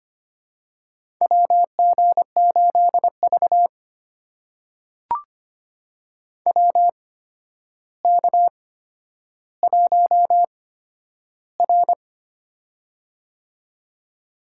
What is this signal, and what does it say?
Morse code, heard as a keyed tone, about 25 wpm, spaced as for 11 wpm, 700 Hz. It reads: WG8V WK1R
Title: USA Call Signs - 4 Characters - 8x Character Spacing - 25wpm